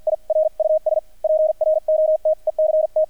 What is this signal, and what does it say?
Morse code, heard as a keyed tone, about 50 wpm, spaced as for 42 wpm, 630 Hz. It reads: IRUS PROTECT
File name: archived / CRSnet